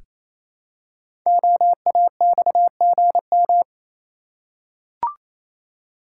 Morse code, heard as a keyed tone, about 27 wpm, 700 Hz. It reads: OAXGM
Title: Code Groups - 5 Letters - 28wpm